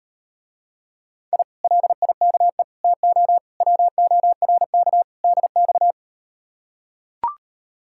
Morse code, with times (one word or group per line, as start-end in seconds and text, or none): 1.3-1.4 I
1.6-2.6 LIKE
2.8-3.4 TO
3.6-5.0 WORK
5.2-5.9 DX